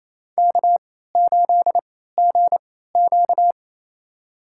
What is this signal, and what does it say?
Morse code, heard as a keyed tone, about 27 wpm, spaced as for 16 wpm, 700 Hz. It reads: K8GQ
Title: USA Call Signs - 4 Characters - 3x Character Spacing - 28wpm